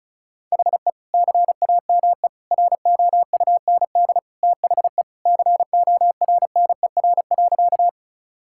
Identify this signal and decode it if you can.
Morse code, heard as a keyed tone, about 36 wpm, 700 Hz. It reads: HE CAME ROUND THE CORNER.
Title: Sentences from Top 1600 Words 35wpm